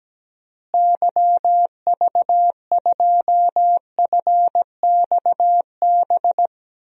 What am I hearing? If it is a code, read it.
Morse code, heard as a keyed tone, about 17 wpm, 700 Hz. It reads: YV2FXB
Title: DXCC-100 Call Signs - 6 Characters - Repeated 3x - 17wpm